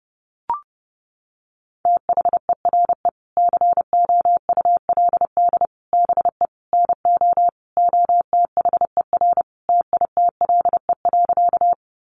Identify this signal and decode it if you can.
Morse code, heard as a keyed tone, about 28 wpm, 700 Hz. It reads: THERE COULD BE NO OTHER TITLE.
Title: Sentences from Top 1200 Words 30wpm